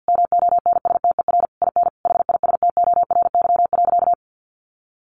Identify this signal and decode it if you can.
Morse code, heard as a keyed tone, about 50 wpm, 700 Hz. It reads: MONSTER IN HISTORY.
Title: A Sound Of Thunder 50wpm